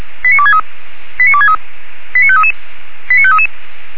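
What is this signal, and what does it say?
Five-tone selective call (ZVEI), heard as quick runs of five tones, tones of about 80 ms. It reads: 89263 89263 89533 89533
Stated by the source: ZVEI five-tone selective call 89347, received by SDR.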